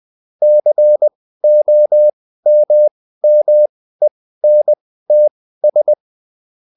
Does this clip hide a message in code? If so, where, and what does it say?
Yes: Morse code from 0.4 to 5.9 s, heard as a keyed tone, about 20 wpm, spaced as for 14 wpm, 600 Hz. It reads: COMMENTS